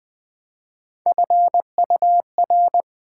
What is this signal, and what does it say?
Morse code, heard as a keyed tone, about 20 wpm, 700 Hz. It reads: FUR
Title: Build A Word - 1 Letter At A Time - 3-Letter Words - 20wpm